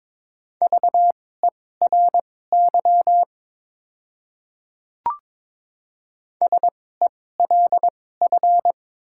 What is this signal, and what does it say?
Morse code, heard as a keyed tone, about 22 wpm, spaced as for 16 wpm, 700 Hz. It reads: VERY SELF
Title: 2x Character Spacing - Top 200 Words - 22wpm